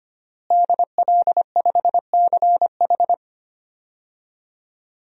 Morse code, heard as a keyed tone, about 25 wpm, 700 Hz. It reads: DL5CH
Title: DXCC-100 Call Signs - 5 Characters - Repeated 3x - 25wpm